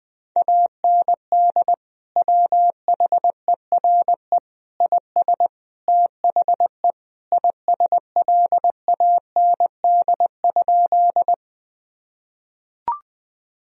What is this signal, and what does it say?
Morse code, heard as a keyed tone, about 20 wpm, 700 Hz. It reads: AND WHERE IS THE ISLAND?
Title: Sentences from Top 1000 Words - Review Part 3 - 20wpm